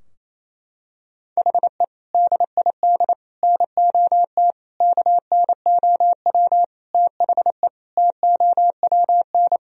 Morse code, heard as a keyed tone, about 28 wpm, 700 Hz. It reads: HE DID NOT KNOW THE TOWN
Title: Sentences from Top 1000 Words - Review Part 2 - 28wpm